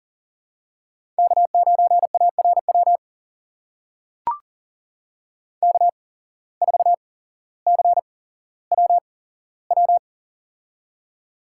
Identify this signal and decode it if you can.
Morse code, heard as a keyed tone, about 42 wpm, spaced as for 17 wpm, 700 Hz. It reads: K9ARW K4CWW
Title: USA Call Signs - 5 Characters - 8x Character Spacing - 40wpm